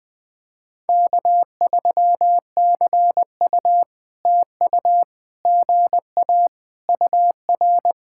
Morse code, heard as a keyed tone, about 20 wpm, 700 Hz. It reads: K3CU TU GA UR
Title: POTA QSO Practice - 20wpm